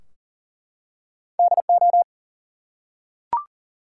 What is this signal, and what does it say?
Morse code, heard as a keyed tone, about 40 wpm, 700 Hz. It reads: DO
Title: Two Letter Words - 40wpm